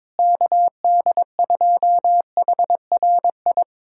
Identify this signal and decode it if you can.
Morse code, heard as a keyed tone, about 22 wpm, 700 Hz. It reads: KD2HRI
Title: USA Callsign DE Callsign 22wpm